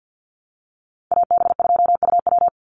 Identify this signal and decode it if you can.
Morse code, heard as a keyed tone, about 50 wpm, 700 Hz. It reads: AB2UW
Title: USA Call Signs 50wpm